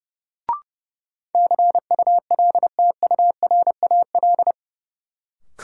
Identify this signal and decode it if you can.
Morse code, heard as a keyed tone, about 30 wpm, 700 Hz. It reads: CULTURAL
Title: Sentences from Top 1500 Words - New Word Drill - 30wpm